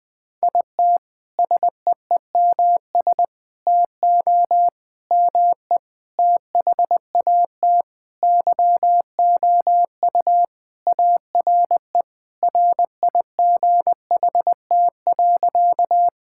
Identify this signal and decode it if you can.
Morse code, heard as a keyed tone, about 20 wpm, 700 Hz. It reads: IT SEEMS TO ME THAT YOU ARE RIGHT.
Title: Sentences with 5-Word Phrases - 20wpm